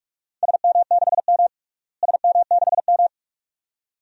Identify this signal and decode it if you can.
Morse code, heard as a keyed tone, about 42 wpm, 700 Hz. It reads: SM6M SM6M
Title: DXCC-100 Call Signs - 4 Characters - Repeated 2x - 45wpm